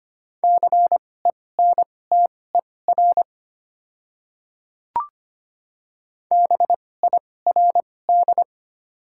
Morse code, heard as a keyed tone, about 25 wpm, spaced as for 18 wpm, 700 Hz. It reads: CENTER BIRD
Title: 2x Character Spacing - Top 500 Words - 25wpm